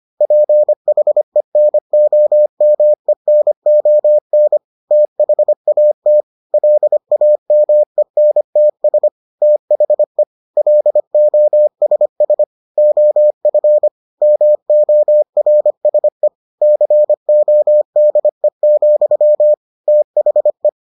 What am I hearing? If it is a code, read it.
Morse code, heard as a keyed tone, about 25 wpm, 600 Hz. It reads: PHENOMENON THAT LAMENTS THE LOSS OF MORSE CODE, THE